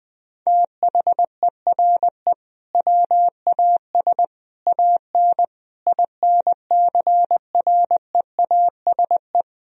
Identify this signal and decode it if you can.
Morse code, heard as a keyed tone, about 21 wpm, 700 Hz. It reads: THERE WAS AN INCREASE
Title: Sentences from Top 1000 Words - Review Part 4 - 20wpm